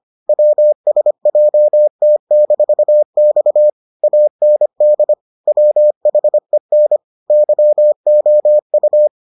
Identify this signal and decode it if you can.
Morse code, heard as a keyed tone, about 25 wpm, 600 Hz. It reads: WSJT-X AND WHEN YOU